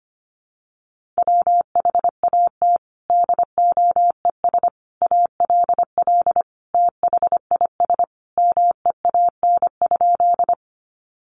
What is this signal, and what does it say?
Morse code, heard as a keyed tone, about 25 wpm, 700 Hz. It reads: WHAT DOES ALL THIS MEAN?
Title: Sentences from Top 200 Words 25wpm